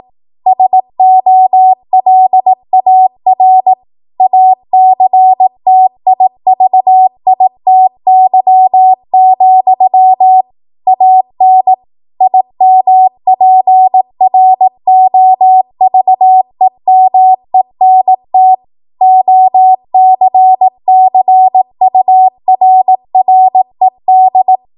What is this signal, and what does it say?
Morse code, heard as a keyed tone, about 18 wpm, 750 Hz. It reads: SOLAR ACTIVITY, AN IMPROVEMENT OCCURRED
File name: Archive / 18 WPM / 18WPM